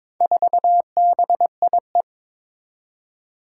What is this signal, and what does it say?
Morse code, heard as a keyed tone, about 22 wpm, 700 Hz. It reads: 4BIE
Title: Four Challenging Letter-Number Combinations 22wpm